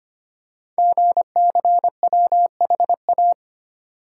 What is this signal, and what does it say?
Morse code, heard as a keyed tone, about 25 wpm, 700 Hz. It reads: GCWHA
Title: Code Groups - 5 Letter:Numbers - 25wpm